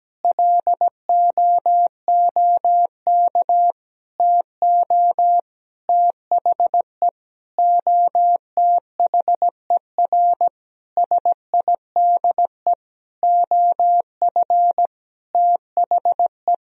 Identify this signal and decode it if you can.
Morse code, heard as a keyed tone, about 17 wpm, 700 Hz. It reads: LOOK TO THE OTHER SIDE OF THE
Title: Sentences from Top 1000 Words - Review Part 1 - 17wpm